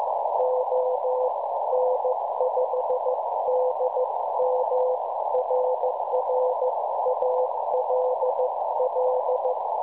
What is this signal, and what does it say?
Morse code, heard as a keyed tone, about 15 wpm, 510 Hz, with noise. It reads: ON5DM RR ALL
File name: cwronde / websdr recording .6kHz